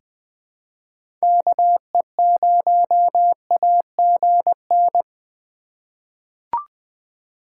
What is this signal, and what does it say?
Morse code, heard as a keyed tone, about 20 wpm, 700 Hz. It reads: KE0AGN
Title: USA Call Signs - 6 Characters - 4x Character Spacing - 20wpm